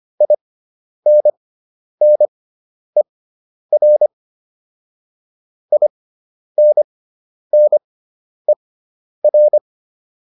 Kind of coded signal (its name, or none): Morse code